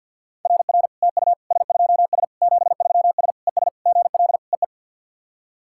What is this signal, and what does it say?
Morse code, heard as a keyed tone, about 50 wpm, 700 Hz. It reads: RR TU S2S 73S ES GL EE
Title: SOTA QSO Practice 50wpm